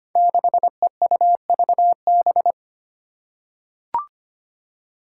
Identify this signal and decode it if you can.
Morse code, heard as a keyed tone, about 25 wpm, 700 Hz. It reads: 6EUVB
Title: Five Challenging Letter-Number Combinations 25wpm